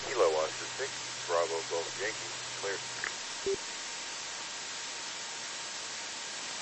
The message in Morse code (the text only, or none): none